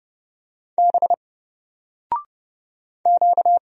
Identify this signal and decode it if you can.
Morse code, heard as a keyed tone, about 29 wpm, 700 Hz. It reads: B Q
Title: Single Letter-Number 30wpm